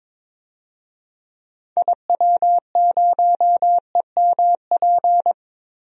Morse code, heard as a keyed tone, about 22 wpm, 700 Hz. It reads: IW0EMP
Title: DXCC-100 Call Signs - 6 Characters - Repeated 3x - 22wpm